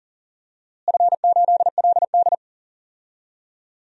Morse code, heard as a keyed tone, about 40 wpm, 690 Hz. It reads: F8LD
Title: DXCC-100 Call Signs - 4 Characters - Repeated 2x - 40wpm